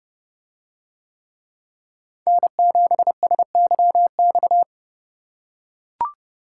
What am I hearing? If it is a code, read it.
Morse code, heard as a keyed tone, about 30 wpm, 700 Hz. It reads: N7SYX